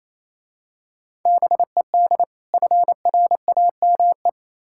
Morse code, heard as a keyed tone, about 27 wpm, 690 Hz. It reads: BED FRAME
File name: Easy Words - Speed-Racing with Hints - 22wpm